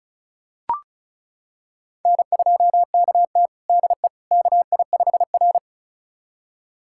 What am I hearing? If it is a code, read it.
Morse code, heard as a keyed tone, about 36 wpm, 700 Hz. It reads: N2KT DE KI5R
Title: USA Callsign DE Callsign - 4 Characters - 35wpm